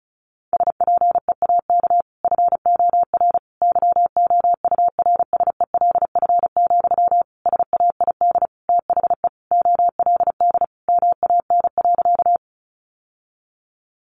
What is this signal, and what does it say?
Morse code, heard as a keyed tone, about 34 wpm, 700 Hz. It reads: SPEAK FOR YOURSELF, SAID THE OLD MAN.
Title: Sentences from Top 1100 Words 35wpm